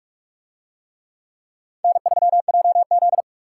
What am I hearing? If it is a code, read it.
Morse code, heard as a keyed tone, about 46 wpm, 700 Hz. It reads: N3JZ